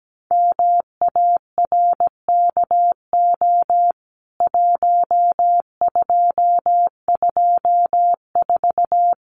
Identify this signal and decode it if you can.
Morse code, heard as a keyed tone, about 17 wpm, 700 Hz. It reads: MARKO 1224
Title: CWT Exchange 17wpm